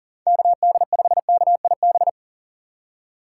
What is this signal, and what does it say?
Morse code, heard as a keyed tone, about 40 wpm, 700 Hz. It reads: KD5KIB
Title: USA Call Signs - 6 Characters - 6x Character Spacing - 40wpm